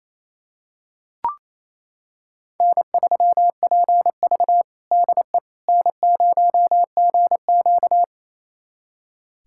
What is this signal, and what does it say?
Morse code, heard as a keyed tone, about 28 wpm, 700 Hz. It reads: N3PV DE N0GQ
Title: USA Callsign DE Callsign - 4 Characters - 28wpm